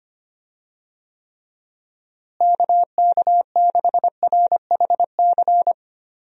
Morse code, heard as a keyed tone, about 25 wpm, 700 Hz. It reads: KK6RHC
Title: USA Call Signs - 6 Characters - Repeated 3x - 25wpm